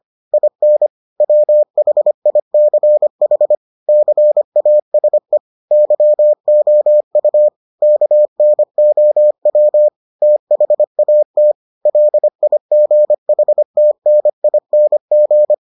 Morse code, heard as a keyed tone, about 25 wpm, 600 Hz. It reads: IN WHICH CASE YOU KNOW THAT LIGHTNING